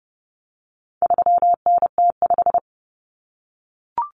Morse code, heard as a keyed tone, about 30 wpm, 700 Hz. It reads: 3NT5
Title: Common Cut Numbers - 4-Digit 30wpm